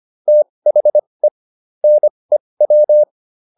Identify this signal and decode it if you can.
Morse code, heard as a keyed tone, about 25 wpm, spaced as for 20 wpm, 600 Hz. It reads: THE NEW
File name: WD204a 20